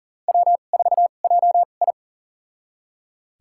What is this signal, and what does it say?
Morse code, heard as a keyed tone, about 40 wpm, spaced as for 29 wpm, 700 Hz. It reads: W4JI
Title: USA Call Signs - 2x Character Spacing - 40wpm